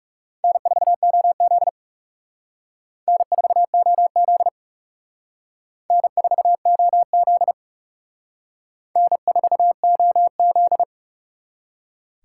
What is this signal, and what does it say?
Morse code, heard as a keyed tone, about 37 wpm, 700 Hz. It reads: N4OZ N4OZ N4OZ N4OZ
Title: Speed Racing - USA 1x2 Call Signs 30wpm